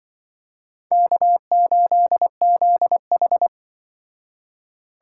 Morse code, heard as a keyed tone, about 25 wpm, 700 Hz. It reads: K8ZH
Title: Speed Racing - USA 1x2 Call Signs 20wpm